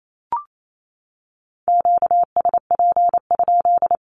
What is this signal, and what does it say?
Morse code, heard as a keyed tone, about 28 wpm, 700 Hz. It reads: QSP?
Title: Q-Codes 28wpm